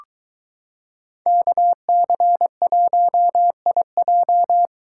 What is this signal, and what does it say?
Morse code, heard as a keyed tone, about 23 wpm, 700 Hz. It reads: KC1IJ